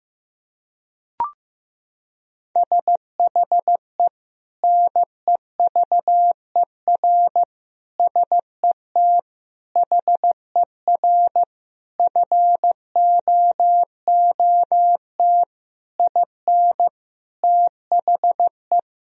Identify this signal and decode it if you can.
Morse code, heard as a keyed tone, about 15 wpm, 700 Hz. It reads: SHE NEVER SET HER FOOT IN THE
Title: Sentences from Top 1000 Words - Review Part 3 - 15wpm